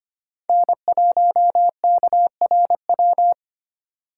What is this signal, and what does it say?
Morse code, heard as a keyed tone, about 25 wpm, 700 Hz. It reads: N1KRW
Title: USA Call Signs - 5 Characters - 6x Character Spacing - 25wpm